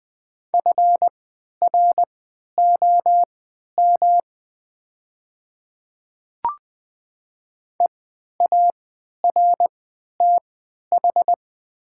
Morse code, heard as a keyed tone, about 20 wpm, spaced as for 11 wpm, 690 Hz. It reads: FROM EARTH
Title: 3x Character Spacing - Top 300 Words - 20wpm